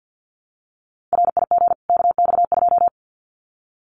Morse code, heard as a keyed tone, about 50 wpm, 700 Hz. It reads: RIG KX2
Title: Common QSO Phrases 50wpm